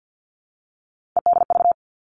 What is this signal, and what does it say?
Morse code, heard as a keyed tone, about 50 wpm, 700 Hz. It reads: EDV
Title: All Cut Numbers - 3-Digit 50wpm